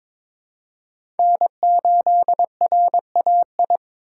Morse code, heard as a keyed tone, about 22 wpm, 700 Hz. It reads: N8RAI
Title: USA Call Signs - 5 Characters - 6x Character Spacing - 22wpm